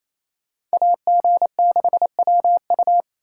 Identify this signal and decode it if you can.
Morse code, heard as a keyed tone, about 36 wpm, 700 Hz. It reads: AG6WU